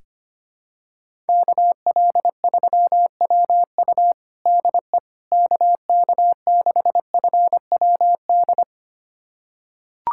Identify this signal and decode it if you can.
Morse code, heard as a keyed tone, about 25 wpm, 700 Hz. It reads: KL3WU DE KK6FWD
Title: USA Callsign DE Callsign 25wpm